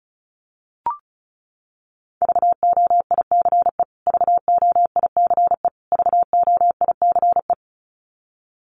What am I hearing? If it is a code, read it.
Morse code, heard as a keyed tone, about 34 wpm, 700 Hz. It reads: VOICE VOICE VOICE
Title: Top 500 Words Repeated 35wpm